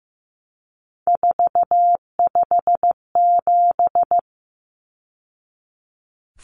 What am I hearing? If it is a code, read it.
Morse code, heard as a keyed tone, about 15 wpm, 700 Hz. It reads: 457
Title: Numbers - 3-Digit 15wpm